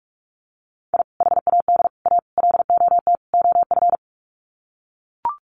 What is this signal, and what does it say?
Morse code, heard as a keyed tone, about 44 wpm, 700 Hz. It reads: I HAD A LOT OF
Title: Sets of 5 Words Encore 45wpm